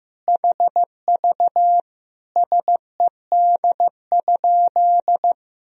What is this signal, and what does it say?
Morse code, heard as a keyed tone, about 15 wpm, 700 Hz. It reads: HV SED?
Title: Sentences from Top 100 Words and Abbreviations 15wpm